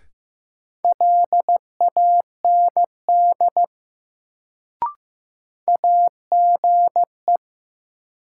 Morse code, heard as a keyed tone, about 15 wpm, 700 Hz. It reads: LAND AGE
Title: Sentences from Top 500 Words - New Word Drill - 15wpm